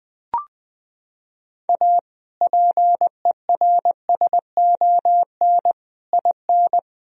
Morse code, heard as a keyed tone, about 20 wpm, 700 Hz. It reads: A PERSON IN
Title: Sentences from Top 1000 Words - Review Part 4 - 20wpm